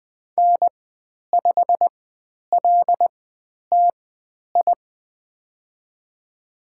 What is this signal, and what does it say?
Morse code, heard as a keyed tone, about 20 wpm, spaced as for 10 wpm, 700 Hz. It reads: N5LTI
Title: 095 Lesson 16g - Letter c - Call Signs 20wpm